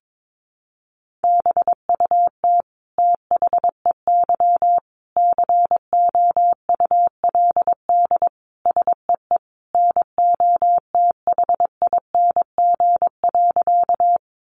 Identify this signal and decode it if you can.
Morse code, heard as a keyed tone, about 22 wpm, 700 Hz. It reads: BUT THEY COULD SEE NOTHING.
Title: Sentences from Top 500 Words 22wpm